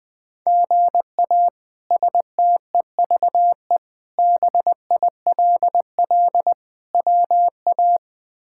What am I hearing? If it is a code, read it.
Morse code, heard as a keyed tone, about 21 wpm, 700 Hz. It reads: GA STEVE BILL WA
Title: K1USN SST QSO Practice - 20wpm